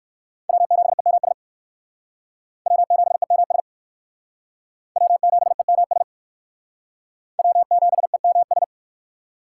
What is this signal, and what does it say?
Morse code, heard as a keyed tone, about 61 wpm, 700 Hz. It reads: W7EMS W7EMS W7EMS W7EMS